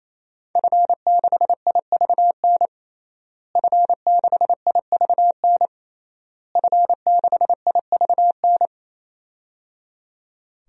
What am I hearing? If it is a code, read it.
Morse code, heard as a keyed tone, about 28 wpm, 690 Hz. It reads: F6IVN F6IVN F6IVN
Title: DXCC-100 Call Signs - 5 Characters - Repeated 3x - 28wpm